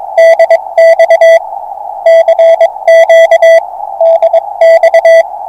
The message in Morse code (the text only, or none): DX CQ DX